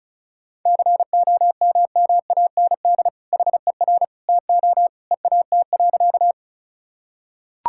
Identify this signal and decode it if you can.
Morse code, heard as a keyed tone, about 36 wpm, 700 Hz. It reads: COMMAND HER TO EAT.
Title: Sentences from Top 1600 Words 35wpm